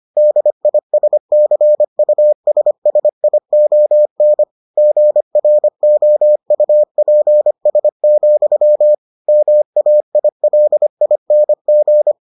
Morse code, heard as a keyed tone, about 25 wpm, 600 Hz. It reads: DISCUSSION GROUPS, MAILING